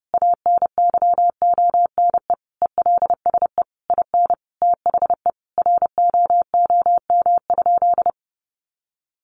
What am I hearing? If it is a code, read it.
Morse code, heard as a keyed tone, about 30 wpm, 700 Hz. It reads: ANYONE ELSE IN THE ROOM?
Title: Sentences from Top 900 Words 30wpm